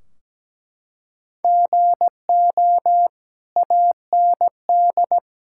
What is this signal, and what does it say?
Morse code, heard as a keyed tone, about 17 wpm, 700 Hz. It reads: GO AND
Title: Sentences from Top 1600 Words 17wpm